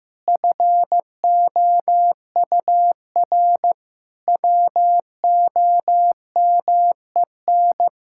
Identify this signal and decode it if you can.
Morse code, heard as a keyed tone, about 15 wpm, 700 Hz. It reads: FOUR WOMEN
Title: Pangram Challenge v2 - 15wpm